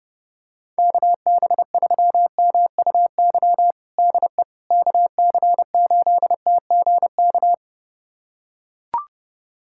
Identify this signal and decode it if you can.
Morse code, heard as a keyed tone, about 30 wpm, 700 Hz. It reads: KB3MUY DE KC8TGK
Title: USA Callsign DE Callsign - 6 Characters - 30wpm